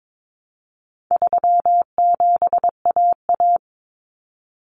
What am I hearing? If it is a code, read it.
Morse code, heard as a keyed tone, about 22 wpm, 700 Hz. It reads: 37AA
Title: Common Cut Numbers - 4-Digit 22wpm